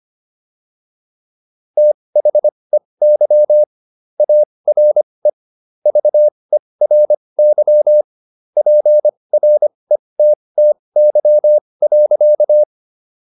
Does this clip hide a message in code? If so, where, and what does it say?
Yes: Morse code from 1.8 to 12.6 s, heard as a keyed tone, about 25 wpm, spaced as for 20 wpm, 600 Hz. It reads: THEY ARE VERY PRETTY.